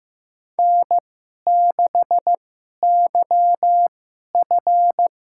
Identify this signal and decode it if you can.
Morse code, heard as a keyed tone, about 15 wpm, spaced as for 11 wpm, 700 Hz. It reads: N6YF
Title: USA Call Signs - 2x Character Spacing - 15wpm